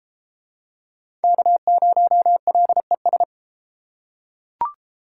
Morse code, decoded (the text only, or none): K0LES